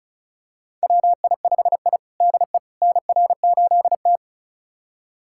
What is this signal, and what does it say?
Morse code, heard as a keyed tone, about 36 wpm, 690 Hz. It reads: WI5I DE NR8T
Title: USA Callsign DE Callsign - 4 Characters - 35wpm